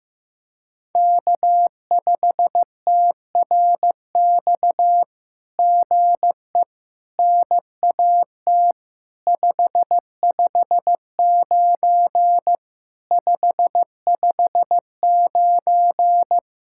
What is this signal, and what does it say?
Morse code, heard as a keyed tone, about 15 wpm, 700 Hz. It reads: K5TRX GE NAT 559 559
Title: SOTA QSO Practice 15wpm